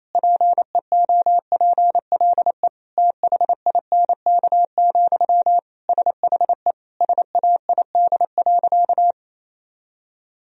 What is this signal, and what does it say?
Morse code, heard as a keyed tone, about 28 wpm, 700 Hz. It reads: PEOPLE THINK, SHE SAID.
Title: Sentences from Top 1000 Words - Review Part 4 - 28wpm